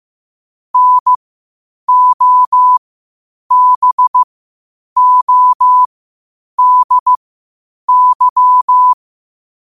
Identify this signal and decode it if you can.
Morse code, heard as a keyed tone, about 15 wpm, spaced as for 8 wpm, 1000 Hz. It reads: NOBODY
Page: Quotes for Fri, 15 Aug 2025 in Morse Code at 8 words per minute.